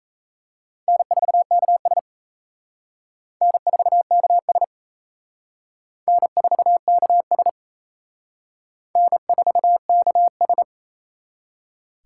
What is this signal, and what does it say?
Morse code, heard as a keyed tone, about 33 wpm, 700 Hz. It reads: N4KS N4KS N4KS N4KS
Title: Speed Racing - USA 1x2 Call Signs 28wpm